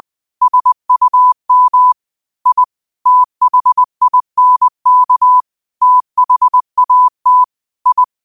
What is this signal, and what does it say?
Morse code, heard as a keyed tone, about 20 wpm, 1000 Hz. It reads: SUM I THINK THAT I